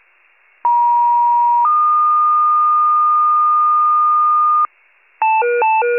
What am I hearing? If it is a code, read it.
Two-tone page, heard as a one-second tone, then a three-second tone, about 950 then 1220 Hz, with noise. It is heard from 0.6 to 4.7 s.
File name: wcalert